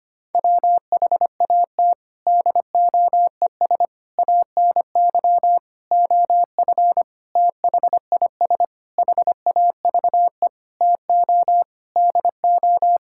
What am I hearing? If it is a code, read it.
Morse code, heard as a keyed tone, about 25 wpm, 700 Hz. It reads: WHAT DOES ANY OF THIS HAVE TO DO
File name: Sets of 9 Words - 25wpm